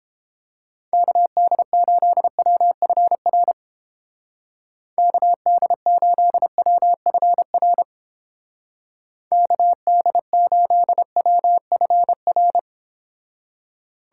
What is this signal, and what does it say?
Morse code, heard as a keyed tone, about 31 wpm, 700 Hz. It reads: KD8WFR KD8WFR KD8WFR